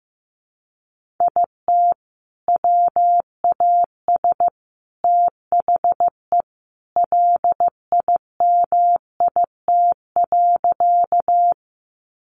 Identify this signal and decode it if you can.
Morse code, heard as a keyed tone, about 15 wpm, 700 Hz. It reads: IT WAS THE LIMIT.
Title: Sentences from Top 1500 Words 15wpm